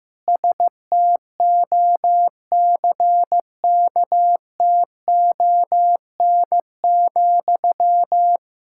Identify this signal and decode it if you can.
Morse code, heard as a keyed tone, about 15 wpm, 700 Hz. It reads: STOCKTON,
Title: Practice QSO - Exchange 1 - Beginning - 15wpm